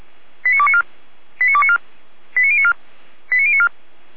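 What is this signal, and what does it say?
Five-tone selective call (ZVEI), heard as quick runs of five tones, tones of about 70 ms. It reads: 89294 89294 89094 89094
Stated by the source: ZVEI five-tone selective call 89270 received on an SDR.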